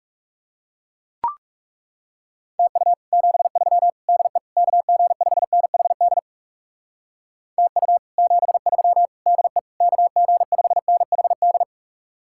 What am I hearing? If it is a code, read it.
Morse code, heard as a keyed tone, about 38 wpm, 700 Hz. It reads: TU 73 DE KG5NHD TU 73 DE KG5NHD
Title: Speed-Racing Common QSO Phrases Encore - 30wpm